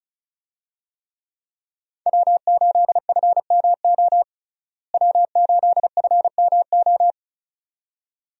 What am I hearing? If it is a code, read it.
Morse code, heard as a keyed tone, about 36 wpm, 700 Hz. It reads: W8FMO W8FMO
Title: USA Call Signs - 5 Characters - Repeated 2x - 35wpm